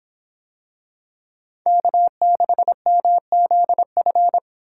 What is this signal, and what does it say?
Morse code, heard as a keyed tone, about 24 wpm, 700 Hz. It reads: K6MZF